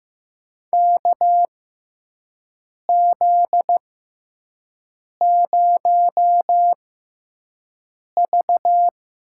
Morse code, heard as a keyed tone, about 15 wpm, spaced as for 5 wpm, 700 Hz. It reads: KZ0V
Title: USA Call Signs - 4 Characters - 6x Character Spacing - 15wpm